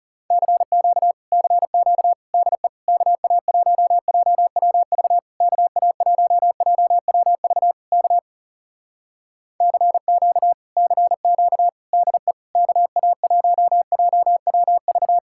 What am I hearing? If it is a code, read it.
Morse code, heard as a keyed tone, about 40 wpm, 700 Hz. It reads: CQ CQ DE KA1JWV KA1JWV K CQ CQ DE KA1JWV